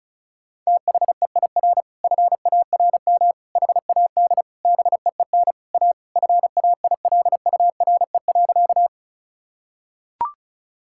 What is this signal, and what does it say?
Morse code, heard as a keyed tone, about 36 wpm, 700 Hz. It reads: THEIR FARM HAD BEEN A FAILURE.
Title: Sentences from Top 1000 Words - Review Part 6 - 35wpm